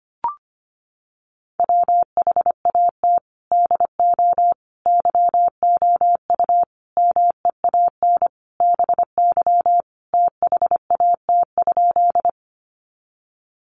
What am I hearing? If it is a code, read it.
Morse code, heard as a keyed tone, about 25 wpm, 700 Hz. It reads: WHAT DO YOU MEAN BY THAT?
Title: Sentences from Top 200 Words 25wpm